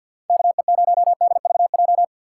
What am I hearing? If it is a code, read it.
Morse code, heard as a keyed tone, about 62 wpm, 700 Hz. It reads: KE0DVJ
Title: Speed Racing - USA 2x3 Call Signs 50wpm